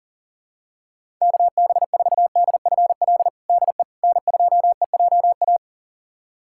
Morse code, heard as a keyed tone, about 40 wpm, 700 Hz. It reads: KB4DFL DE N2EJA
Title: USA Callsign DE Callsign 40wpm